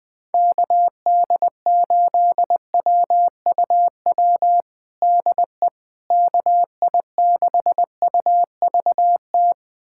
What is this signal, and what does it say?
Morse code, heard as a keyed tone, about 20 wpm, 700 Hz. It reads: KD8WUW DE KI6UVT
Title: USA Callsign DE Callsign - 6 Characters - 20wpm